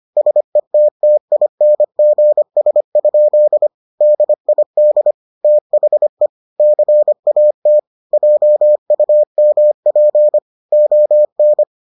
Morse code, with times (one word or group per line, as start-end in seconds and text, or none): 0.2-3.7 SETTINGS?
4.0-5.1 DID
5.4-6.3 THE
6.6-7.8 CAT
8.1-10.4 JUMP
10.7-11.6 ON